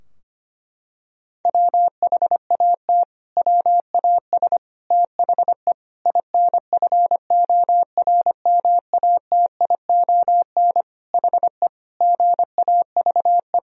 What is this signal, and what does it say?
Morse code, heard as a keyed tone, about 25 wpm, 700 Hz. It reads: WHAT WAS THE INFORMATION HE GAVE
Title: Sentences from Top 1000 Words - Review Part 3 - 25wpm